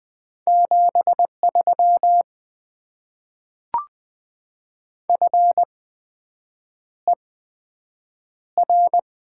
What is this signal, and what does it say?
Morse code, heard as a keyed tone, about 20 wpm, spaced as for 7 wpm, 700 Hz. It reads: 73 FER
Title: Top 20 QSO Elements - 8x Character Spacing - 20wpm